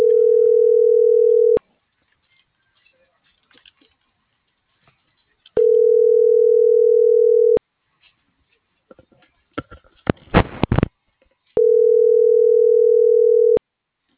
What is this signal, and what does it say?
Ringback tone (North America), 440 + 480 Hz.